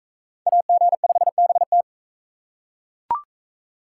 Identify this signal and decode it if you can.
Morse code, heard as a keyed tone, about 42 wpm, 700 Hz. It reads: AG5BT